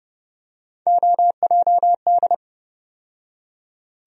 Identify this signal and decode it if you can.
Morse code, heard as a keyed tone, about 30 wpm, 700 Hz. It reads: OJD
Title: USA Call Sign Postfixes - 3 Letters 30wpm